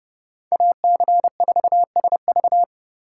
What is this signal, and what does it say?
Morse code, heard as a keyed tone, about 30 wpm, 700 Hz. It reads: AC4SV